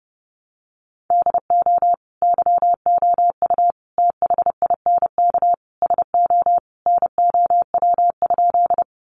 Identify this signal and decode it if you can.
Morse code, heard as a keyed tone, about 28 wpm, 700 Hz. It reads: DO YOU THINK SO NOW?
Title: Sentences from Top 200 Words 30wpm